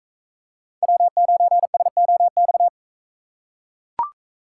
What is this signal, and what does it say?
Morse code, heard as a keyed tone, about 40 wpm, 700 Hz. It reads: W9SOX